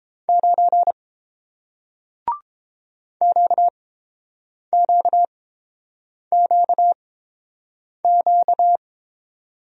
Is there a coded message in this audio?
Yes: Morse code from 0.3 to 8.8 s, heard as a keyed tone, about 30 wpm, spaced as for 7 wpm, 700 Hz. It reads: 9 QQQQ